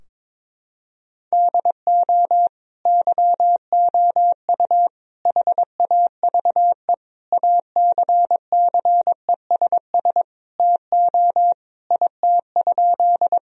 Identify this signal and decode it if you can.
Morse code, heard as a keyed tone, about 22 wpm, 700 Hz. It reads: DO YOU HAVE ACCESS TO IT?
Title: Sentences from Top 1000 Words - Review Part 4 - 22wpm